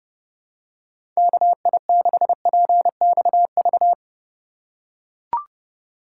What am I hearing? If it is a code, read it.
Morse code, heard as a keyed tone, about 32 wpm, 700 Hz. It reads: KI6PXV
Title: USA Call Signs - 6 Characters - 3x Character Spacing - 30wpm